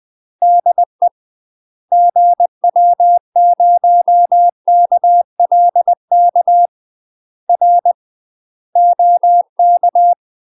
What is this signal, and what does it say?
Morse code, heard as a keyed tone, about 20 wpm, 700 Hz. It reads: DE GW0KLK R OK
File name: UK MASTER QSOs TRAINING/20WPM